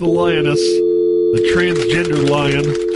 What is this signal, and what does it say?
Telephone call progress tone (North America), heard as a telephone network's tone, about 350, 440 Hz, with speech. Dial tone.